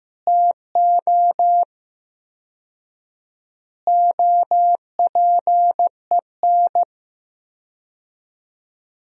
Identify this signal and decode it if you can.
Morse code, heard as a keyed tone, about 16 wpm, 700 Hz. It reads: TO OPEN
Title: 4x Word-Spacing - Sentences from Top 300 Words - 15wpm